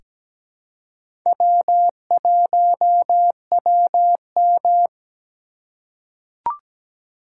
Morse code, heard as a keyed tone, about 17 wpm, 700 Hz. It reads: W1WM
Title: USA Call Signs - 4 Characters - 4x Character Spacing - 17wpm